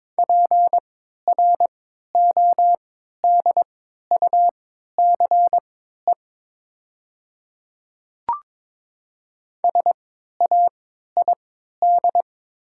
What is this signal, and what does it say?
Morse code, heard as a keyed tone, about 22 wpm, spaced as for 12 wpm, 700 Hz. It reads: PRODUCE SAID
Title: 3x Character Spacing - Top 500 Words - 22wpm